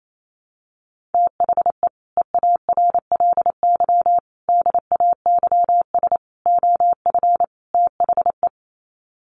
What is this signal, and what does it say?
Morse code, heard as a keyed tone, about 28 wpm, 700 Hz. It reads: THE EARLY DAYS OF THE
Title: Sets of 5 Words Encore 28wpm